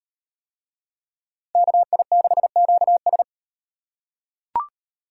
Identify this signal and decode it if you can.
Morse code, heard as a keyed tone, about 34 wpm, 700 Hz. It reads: KI6QS